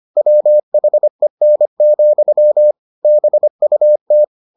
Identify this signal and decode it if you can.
Morse code, heard as a keyed tone, about 25 wpm, 600 Hz. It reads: WHEN, BUT